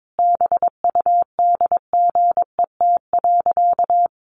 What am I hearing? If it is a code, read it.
Morse code, heard as a keyed tone, about 22 wpm, 700 Hz. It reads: BUDGET.
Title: Sentences from Top 1200 Words 22wpm